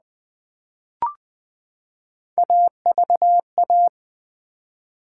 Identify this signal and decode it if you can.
Morse code, heard as a keyed tone, about 20 wpm, 700 Hz. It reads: AVA
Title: Build A Word - 3 Letters At A Time - 9-Letter Words - 20wpm